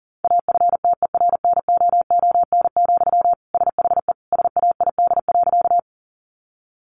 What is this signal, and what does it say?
Morse code, heard as a keyed tone, about 40 wpm, 700 Hz. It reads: AFTERNOON, SHE SAID.
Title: Sentences from Top 1500 Words 40wpm